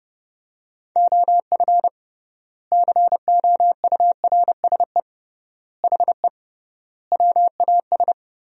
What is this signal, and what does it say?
Morse code, heard as a keyed tone, about 30 wpm, 700 Hz. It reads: OF COURSE HE WAS
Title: 3x Word-Spacing - Sentences from Top 500 Words - 30wpm